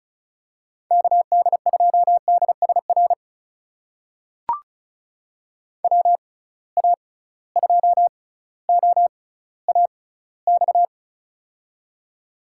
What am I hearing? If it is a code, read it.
Morse code, heard as a keyed tone, about 34 wpm, spaced as for 18 wpm, 700 Hz. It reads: KD2DSR WA2OAX
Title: USA Call Signs - 6 Characters - 6x Character Spacing - 35wpm